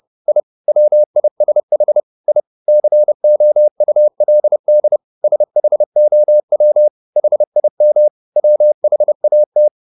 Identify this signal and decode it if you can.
Morse code, heard as a keyed tone, about 30 wpm, 600 Hz. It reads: I WISH I COULD SHOW HIM WHAT
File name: the old man and the sea / the old man and the sea 1 30wpm